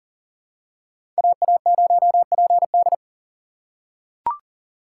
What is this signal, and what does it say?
Morse code, heard as a keyed tone, about 40 wpm, 700 Hz. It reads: AA0PD
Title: USA Call Signs - 6x Character Spacing - 40wpm